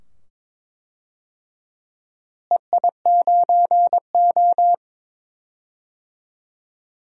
Morse code, heard as a keyed tone, about 22 wpm, 700 Hz. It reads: EI9O
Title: DXCC-100 Call Signs - 4 Characters - Repeated 2x - 22wpm